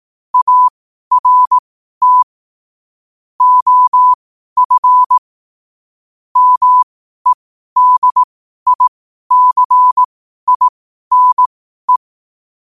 Morse code, heard as a keyed tone, about 18 wpm, spaced as for 12 wpm, 1000 Hz. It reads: ART OF MEDICINE
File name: 12wpm